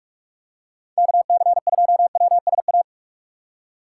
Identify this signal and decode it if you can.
Morse code, heard as a keyed tone, about 46 wpm, 700 Hz. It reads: KC2WSA
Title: USA Call Signs - 6x Character Spacing - 45wpm